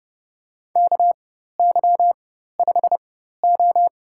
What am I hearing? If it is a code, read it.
Morse code, heard as a keyed tone, about 30 wpm, spaced as for 14 wpm, 700 Hz. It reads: KY5O
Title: USA Call Signs - 4 Characters - 4x Character Spacing - 30wpm